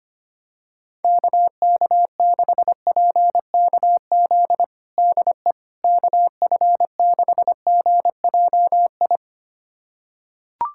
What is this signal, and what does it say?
Morse code, heard as a keyed tone, about 25 wpm, 700 Hz. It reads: KK6PKZ DE KF6GJI
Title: USA Callsign DE Callsign 25wpm